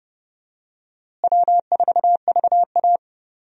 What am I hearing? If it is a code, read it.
Morse code, heard as a keyed tone, about 30 wpm, 700 Hz. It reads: W4VA
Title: USA Call Signs - 4 Characters - Repeated 3x - 30wpm